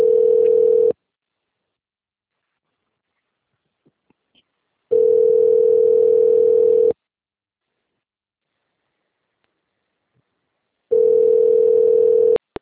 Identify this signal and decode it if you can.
Ringback tone (North America), 440 + 480 Hz.